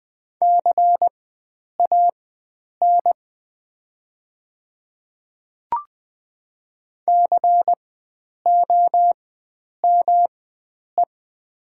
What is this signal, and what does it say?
Morse code, heard as a keyed tone, about 20 wpm, spaced as for 9 wpm, 700 Hz. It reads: CAN COME
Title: 4x Character Spacing - Top 100 Words - 20wpm